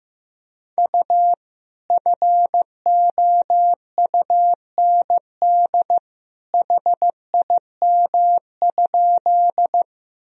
Morse code, heard as a keyed tone, about 15 wpm, 700 Hz. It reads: U FOUND HIM?
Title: Sentences from Top 300 Words and Abbreviations 15wpm